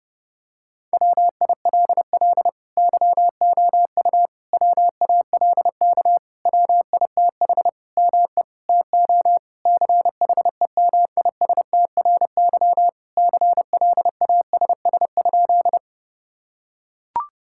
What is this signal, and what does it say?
Morse code, heard as a keyed tone, about 30 wpm, 700 Hz. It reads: WILL YOU WALK WITH ME TO CHEMISTRY CLASS?
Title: Sentences from Top 1000 Words - Review Part 6 - 30wpm